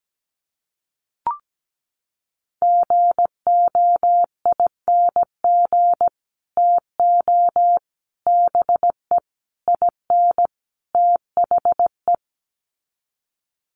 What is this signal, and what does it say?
Morse code, heard as a keyed tone, about 17 wpm, 700 Hz. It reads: GOING TO BE IN THE
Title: Sets of 5 Words Encore 17wpm